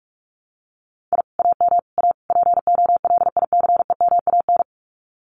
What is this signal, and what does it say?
Morse code, heard as a keyed tone, about 44 wpm, 700 Hz. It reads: I AM A POLICEMAN
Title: Common QSO Phrases Encore 45wpm